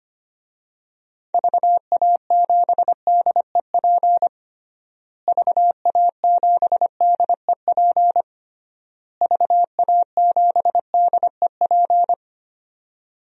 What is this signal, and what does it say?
Morse code, heard as a keyed tone, about 25 wpm, 700 Hz. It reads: VA7DEP VA7DEP VA7DEP
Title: DXCC-100 Call Signs - 6 Characters - Repeated 3x - 25wpm